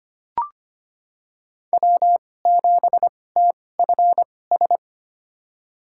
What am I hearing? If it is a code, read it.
Morse code, heard as a keyed tone, about 25 wpm, spaced as for 18 wpm, 700 Hz. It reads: W7TFS
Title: USA Call Signs - 2x Character Spacing - 25wpm